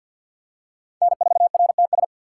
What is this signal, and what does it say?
Morse code, heard as a keyed tone, about 53 wpm, 700 Hz. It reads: N4RTS
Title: USA Call Signs - 5 Characters - 4x Character Spacing - 50wpm